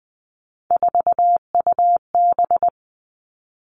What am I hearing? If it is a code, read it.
Morse code, heard as a keyed tone, about 20 wpm, 700 Hz. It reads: 4UB
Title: All Cut Numbers - 3-Digit 20wpm